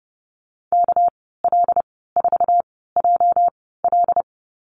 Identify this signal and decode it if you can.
Morse code, heard as a keyed tone, about 30 wpm, spaced as for 17 wpm, 700 Hz. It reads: KL4JL
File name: Lesson 37g - Letter x - Call Signs 30wpm